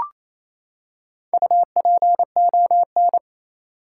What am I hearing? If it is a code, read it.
Morse code, heard as a keyed tone, about 28 wpm, 700 Hz. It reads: UPON
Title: Four Letter Words - 28wpm